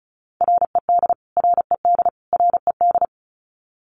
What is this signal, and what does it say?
Morse code, heard as a keyed tone, about 34 wpm, 700 Hz. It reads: RED RED RED
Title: Top 500 Words Repeated 35wpm